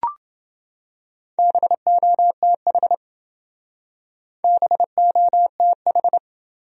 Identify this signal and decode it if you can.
Morse code, heard as a keyed tone, about 28 wpm, 700 Hz. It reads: BOTH BOTH